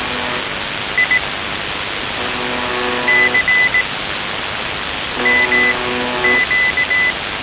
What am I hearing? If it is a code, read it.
Morse code, heard as a keyed tone, about 18 wpm, spaced as for 6 wpm, 2000 Hz, with noise. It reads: IC MQ
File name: s28-4625usb- - -cw-bykompot